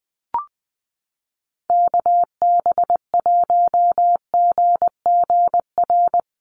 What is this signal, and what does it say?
Morse code, heard as a keyed tone, about 20 wpm, 700 Hz. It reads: KB1GGR
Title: USA Call Signs Repeated 3x 20wpm